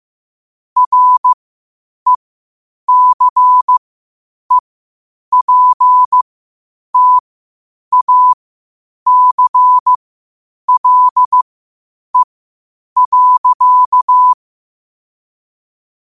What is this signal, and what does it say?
Morse code, heard as a keyed tone, about 15 wpm, spaced as for 8 wpm, 1000 Hz. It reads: RECEPTACLE.
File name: qotdfiles / 8wpm